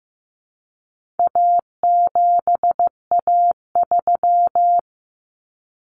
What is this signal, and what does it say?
Morse code, heard as a keyed tone, about 15 wpm, 700 Hz. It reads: A7A3
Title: Common Cut Numbers - 4-Digit 15wpm